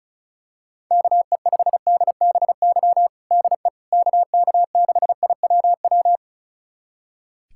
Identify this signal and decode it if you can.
Morse code, heard as a keyed tone, about 34 wpm, 700 Hz. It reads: KE5DBY DE KK6IWW
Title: USA Callsign DE Callsign - 6 Characters - 35wpm